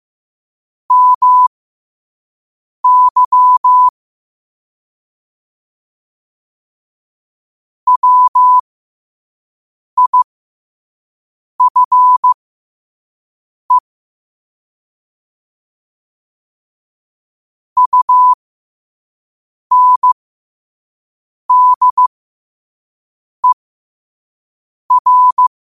Morse code, heard as a keyed tone, about 15 wpm, spaced as for 5 wpm, 1000 Hz. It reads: MY WIFE UNDER
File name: qotdfiles / 5wpm